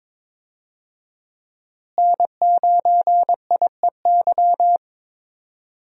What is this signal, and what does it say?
Morse code, heard as a keyed tone, about 22 wpm, 700 Hz. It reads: N9IEY
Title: USA Call Signs - 5 Characters - Repeated 2x - 22wpm